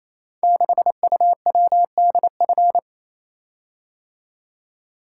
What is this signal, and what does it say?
Morse code, heard as a keyed tone, about 28 wpm, 700 Hz. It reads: 6UWDF
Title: Code Groups - 5 Letter:Numbers - 28wpm